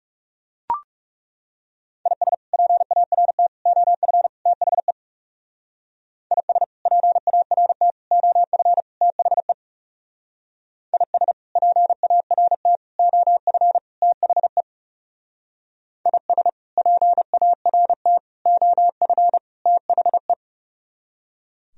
Morse code, heard as a keyed tone, about 38 wpm, 700 Hz. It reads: IS PART OF THE IS PART OF THE IS PART OF THE IS PART OF THE